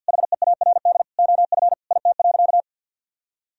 Morse code, heard as a keyed tone, about 52 wpm, 700 Hz. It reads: HEARD OF IT.
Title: Sentences from Top 1000 Words - Review Part 3 - 50wpm